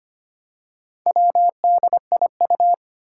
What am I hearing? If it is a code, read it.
Morse code, heard as a keyed tone, about 25 wpm, 700 Hz. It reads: WDIU
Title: Four Challenging Letter-Number Combinations 25wpm